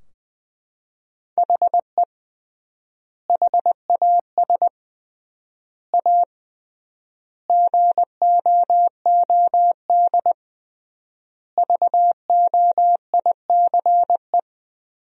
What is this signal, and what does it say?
Morse code, heard as a keyed tone, about 20 wpm, 700 Hz. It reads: HE HAS A GOOD VOICE
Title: 3x Word-Spacing - Sentences from Top 500 Words - 20wpm